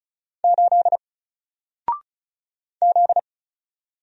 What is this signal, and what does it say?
Morse code, heard as a keyed tone, about 35 wpm, 700 Hz. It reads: 8 Z
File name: Single-Letter-Number-35wpm